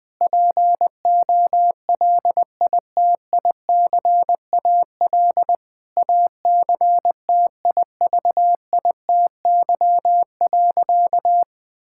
Morse code, heard as a keyed tone, about 20 wpm, 700 Hz. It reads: POLITICAL ACTIVITY.
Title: Sentences from Top 1000 Words - Review Part 5 - 20wpm